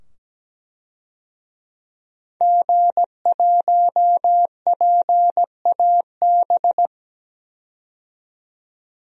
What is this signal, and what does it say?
Morse code, heard as a keyed tone, about 17 wpm, 700 Hz. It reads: G1PAB
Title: DXCC-100 Call Signs - 5 Characters - 17wpm